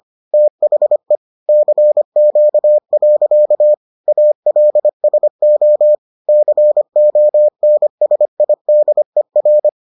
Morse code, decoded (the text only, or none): THE CQ. ALSO CONSIDER